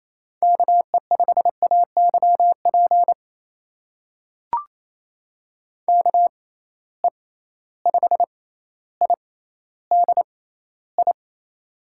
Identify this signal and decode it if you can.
Morse code, heard as a keyed tone, about 28 wpm, spaced as for 14 wpm, 700 Hz. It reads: KE5AYP KE5IDI
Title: USA Call Signs - 6 Characters - 6x Character Spacing - 28wpm